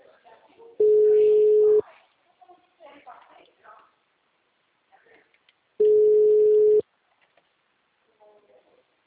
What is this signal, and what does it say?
Ringback tone (Europe, 425 Hz), with speech.